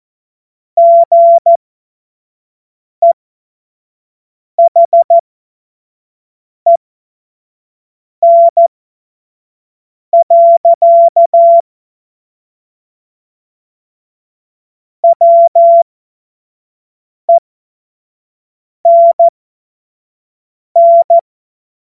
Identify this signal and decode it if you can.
Morse code, heard as a keyed tone, about 14 wpm, spaced as for 5 wpm, 680 Hz. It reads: GEHEN. WENN
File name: Morsekurs / Witze / Witz 10.Oktober 2023 25 F70